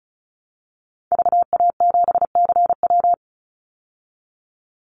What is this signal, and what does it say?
Morse code, heard as a keyed tone, about 34 wpm, 700 Hz. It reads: VA7CW
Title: Call Signs Repeated 3x - 35wpm